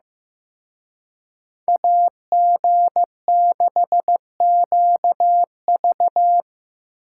Morse code, heard as a keyed tone, about 18 wpm, 700 Hz. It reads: AG6QV
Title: Speed Racing - USA 2x2 Call Signs 15wpm